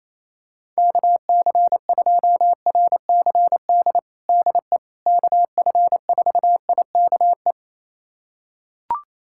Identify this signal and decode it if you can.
Morse code, heard as a keyed tone, about 28 wpm, 700 Hz. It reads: KC2RCD DE KF4IKE
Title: USA Callsign DE Callsign 28wpm